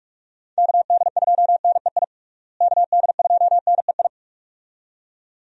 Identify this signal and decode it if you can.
Morse code, heard as a keyed tone, about 46 wpm, 700 Hz. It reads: KD2NEI KD2NEI
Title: USA Call Signs - 6 Characters - Repeated 2x - 45wpm